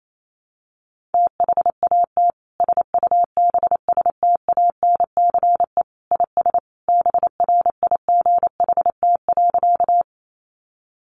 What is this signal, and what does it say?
Morse code, heard as a keyed tone, about 28 wpm, 700 Hz. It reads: THAT SUBSTANCE IS BRIGHT.
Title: Sentences from Top 1500 Words 28wpm